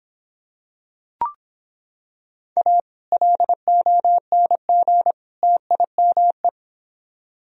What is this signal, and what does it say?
Morse code, heard as a keyed tone, about 26 wpm, 700 Hz. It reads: A LONG TIME